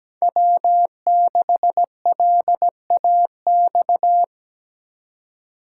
Morse code, heard as a keyed tone, about 17 wpm, 700 Hz. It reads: W6LAX
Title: K1USN SST QSO Practice - 17wpm